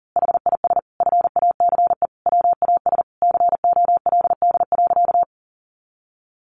Morse code, heard as a keyed tone, about 40 wpm, 700 Hz. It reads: HIS FACE WAS COLD.
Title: A Sound Of Thunder 40wpm